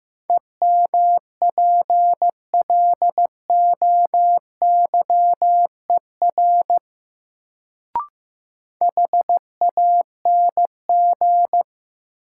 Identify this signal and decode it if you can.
Morse code, heard as a keyed tone, about 15 wpm, 700 Hz. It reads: EMPLOYER HANG
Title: Sentences from Top 1600 Words - New Word Drill - 15wpm